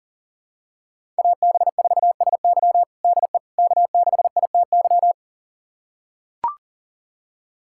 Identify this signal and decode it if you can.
Morse code, heard as a keyed tone, about 40 wpm, 700 Hz. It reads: AB4SY DE K6ITY
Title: USA Callsign DE Callsign - 5 Characters - 40wpm